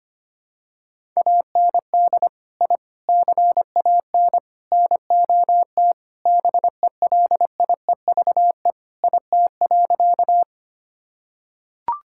Morse code, heard as a keyed tone, about 25 wpm, 700 Hz. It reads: AND I CAN NOT BELIEVE IT.
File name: practice / 25wpm / Sentences with 4-Word Phrases - 25wpm